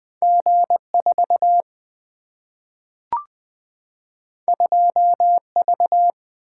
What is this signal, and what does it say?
Morse code, heard as a keyed tone, about 20 wpm, 700 Hz. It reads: G4 2V
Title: Single-Letter Single-Number 20wpm